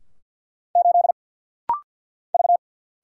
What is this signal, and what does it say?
Morse code, heard as a keyed tone, about 50 wpm, 700 Hz. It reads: 8 V
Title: Single Letter-Number Alt - Rapid-Fire 50wpm